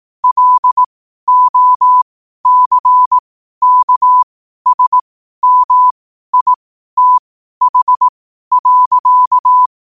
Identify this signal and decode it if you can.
Morse code, heard as a keyed tone, about 18 wpm, spaced as for 12 wpm, 1000 Hz. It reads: LOCKSMITH.
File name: qotdfiles / 12wpm